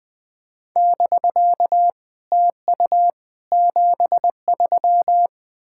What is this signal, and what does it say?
Morse code, heard as a keyed tone, about 20 wpm, 700 Hz. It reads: <BK> TU 73
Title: SOTA QSO Practice 20wpm